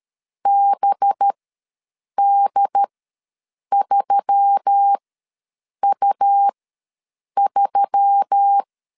Morse code, heard as a keyed tone, about 13 wpm, spaced as for 7 wpm, 800 Hz. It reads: BD3U3
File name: LEZIONI / 5 LBVC / 5 LB ws 12wpm 3034